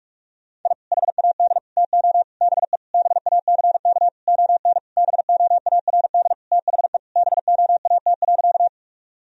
Morse code, heard as a keyed tone, about 46 wpm, 700 Hz. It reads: I HAD TO BE BACK ON BOARD THE BOAT.
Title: Sentences from Top 1000 Words - Review Part 4 - 45wpm